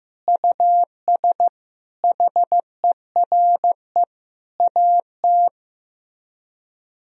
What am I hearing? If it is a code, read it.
Morse code, heard as a keyed tone, about 15 wpm, 700 Hz. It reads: US HERE AT
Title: Speed-Racing - Sets of 5 Words 15wpm